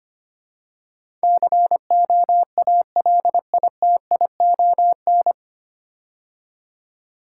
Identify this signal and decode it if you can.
Morse code, heard as a keyed tone, about 25 wpm, 700 Hz. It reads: COALITION
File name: practice / 25wpm / 9 Letter Words - First 3 Letters Given - 25wpm